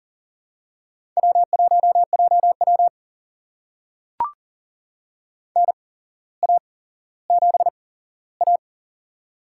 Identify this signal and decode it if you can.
Morse code, heard as a keyed tone, about 40 wpm, spaced as for 17 wpm, 700 Hz. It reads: W1JW NA7A
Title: USA Call Signs - 4 Characters - 8x Character Spacing - 40wpm